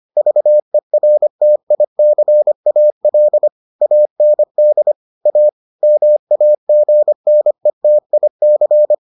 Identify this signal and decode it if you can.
Morse code, heard as a keyed tone, about 25 wpm, 600 Hz. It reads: VERTICAL AND A MAGNETIC